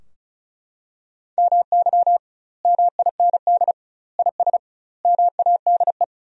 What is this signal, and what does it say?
Morse code, heard as a keyed tone, about 34 wpm, 700 Hz. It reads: MY MIND IS MADE
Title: 2x Word-Spacing - Sentences from Top 500 Words - 35wpm